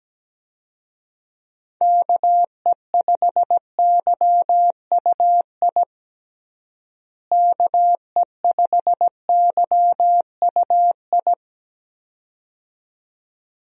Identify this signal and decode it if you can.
Morse code, heard as a keyed tone, about 17 wpm, 700 Hz. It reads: KE5YUI KE5YUI
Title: USA Call Signs - 6 Characters - Repeated 2x - 17wpm